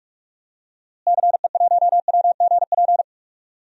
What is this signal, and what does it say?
Morse code, heard as a keyed tone, about 46 wpm, 700 Hz. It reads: CE1WGP
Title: DXCC-100 Call Signs - 6 Characters - Repeated 3x - 45wpm